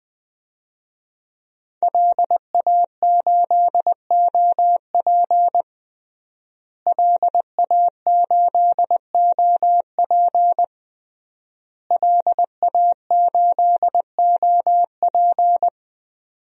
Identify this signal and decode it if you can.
Morse code, heard as a keyed tone, about 20 wpm, 700 Hz. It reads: LA8OP LA8OP LA8OP
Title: DXCC-100 Call Signs - 5 Characters - Repeated 3x - 20wpm